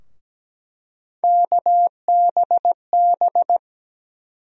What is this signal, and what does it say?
Morse code, heard as a keyed tone, about 17 wpm, 700 Hz. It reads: KBB
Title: USA Call Sign Postfixes - 3 Letters 17wpm